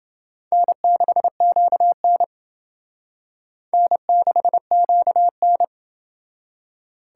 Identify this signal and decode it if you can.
Morse code, heard as a keyed tone, about 28 wpm, 700 Hz. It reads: N6QN N6QN